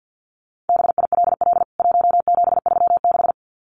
Morse code, heard as a keyed tone, about 48 wpm, 700 Hz. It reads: BILL 1736
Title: CWT Exchange 50wpm